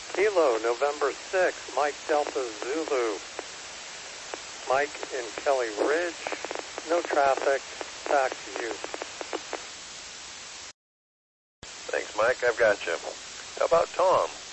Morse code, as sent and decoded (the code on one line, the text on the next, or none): none